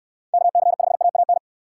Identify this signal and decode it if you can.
Morse code, heard as a keyed tone, about 64 wpm, 700 Hz. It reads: KC5AAN